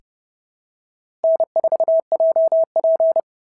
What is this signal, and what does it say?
Morse code, heard as a keyed tone, about 30 wpm, 650 Hz. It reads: N4JP
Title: SOTA QSO Practice 30wpm